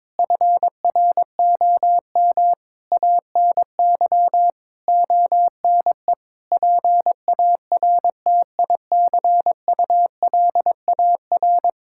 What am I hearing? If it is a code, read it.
Morse code, heard as a keyed tone, about 22 wpm, 700 Hz. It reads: FROM ANY ONE PARTICULAR
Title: Sentences from Top 1000 Words - Review Part 5 - 22wpm